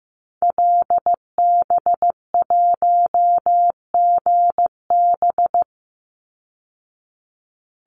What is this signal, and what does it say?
Morse code, heard as a keyed tone, about 15 wpm, 700 Hz. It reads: LB1GB
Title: Call Signs Repeated 3x - 15wpm